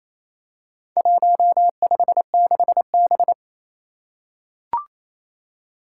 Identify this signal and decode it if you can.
Morse code, heard as a keyed tone, about 28 wpm, 700 Hz. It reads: 156B
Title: Four Challenging Letter-Number Combinations 28wpm